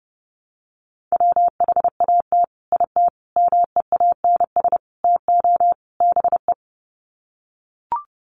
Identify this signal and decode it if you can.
Morse code, heard as a keyed tone, about 30 wpm, 700 Hz. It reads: WHAT IT MEANS TO BE
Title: Sets of 5 Words 30wpm